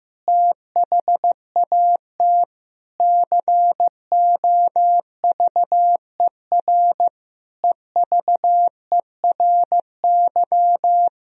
Morse code, heard as a keyed tone, about 15 wpm, 700 Hz. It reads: THAT COVER EVERY
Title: Sentences from Top 1000 Words - Review Part 1 - 15wpm